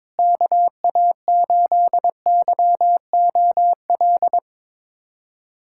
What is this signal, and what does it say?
Morse code, heard as a keyed tone, about 22 wpm, 700 Hz. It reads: KA8YOL
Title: USA Callsign DE Callsign - 6 Characters - 22wpm